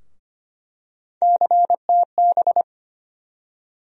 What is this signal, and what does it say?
Morse code, heard as a keyed tone, about 25 wpm, 700 Hz. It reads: CTB
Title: USA Call Sign Postfixes - 3 Letters 25wpm